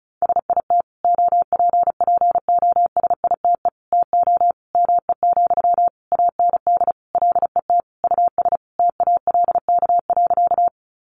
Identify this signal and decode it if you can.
Morse code, heard as a keyed tone, about 34 wpm, 700 Hz. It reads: SIT OPPOSITE TO ME, AND LET US TALK.
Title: Sentences from Top 1300 Words 35wpm